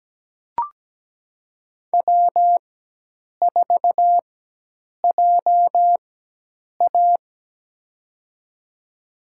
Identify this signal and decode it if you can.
Morse code, heard as a keyed tone, about 17 wpm, spaced as for 8 wpm, 700 Hz. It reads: W4JA
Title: USA Call Signs - 4x Character Spacing - 17wpm